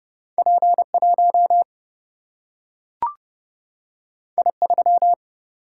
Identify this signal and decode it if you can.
Morse code, heard as a keyed tone, about 30 wpm, 700 Hz. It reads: P 1 I 3
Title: Single-Letter Single-Number 30wpm